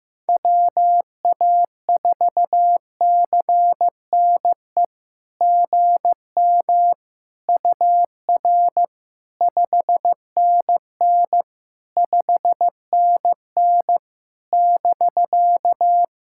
Morse code, heard as a keyed tone, about 15 wpm, 700 Hz. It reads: WA4CNE GM UR 5NN 5NN <BK>
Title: SOTA QSO Practice 15wpm